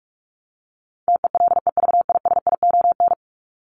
Morse code, heard as a keyed tone, about 44 wpm, 700 Hz. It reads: TELEVISION
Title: Top 501-750 Words 45wpm